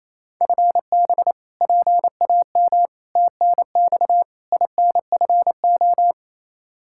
Morse code, heard as a keyed tone, about 28 wpm, 700 Hz. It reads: FB PAM TNX INFO
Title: K1USN SST QSO Practice 28wpm